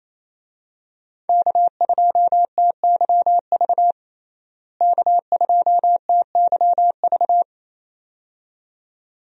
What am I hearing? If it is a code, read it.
Morse code, heard as a keyed tone, about 28 wpm, 700 Hz. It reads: K2TYV K2TYV
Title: USA Call Signs - 5 Characters - Repeated 2x - 28wpm